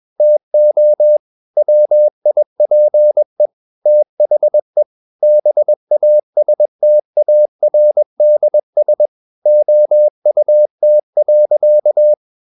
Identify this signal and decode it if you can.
Morse code, heard as a keyed tone, about 21 wpm, 600 Hz. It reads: TO WIPE THE BASTARDS OUT.
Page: CW med Gnister